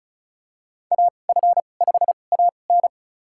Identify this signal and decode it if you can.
Morse code, heard as a keyed tone, about 36 wpm, spaced as for 25 wpm, 700 Hz. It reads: AF5AN
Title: USA Call Signs - 5 Characters - 2x Character Spacing - 35wpm